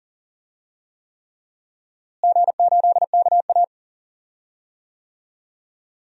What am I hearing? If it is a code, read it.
Morse code, heard as a keyed tone, about 39 wpm, 700 Hz. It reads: G8KA
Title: DXCC-100 Call Signs - 4 Characters - Repeated 3x - 40wpm